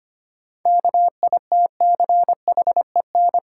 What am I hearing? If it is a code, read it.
Morse code, heard as a keyed tone, about 25 wpm, 700 Hz. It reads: KITCHEN
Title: Sentences from Top 1400 Words - New Word Drill - 25wpm